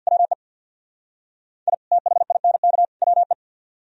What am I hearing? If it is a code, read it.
Morse code, heard as a keyed tone, about 65 wpm, 700 Hz. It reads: WE I THINK WE